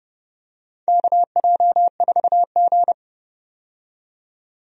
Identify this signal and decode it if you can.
Morse code, heard as a keyed tone, about 31 wpm, 700 Hz. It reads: KJ4G
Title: USA Call Signs - 4 Characters - 30wpm